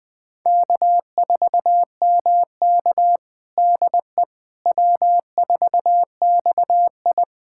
Morse code, heard as a keyed tone, about 20 wpm, 700 Hz. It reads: K4MK DE W4XI
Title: USA Callsign DE Callsign - 4 Characters - 20wpm